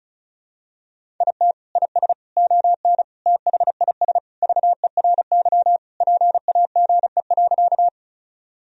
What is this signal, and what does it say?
Morse code, heard as a keyed tone, about 36 wpm, 700 Hz. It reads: IT IS ON THIS VERY PAGE.
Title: Sentences from Top 1000 Words - Review Part 1 - 35wpm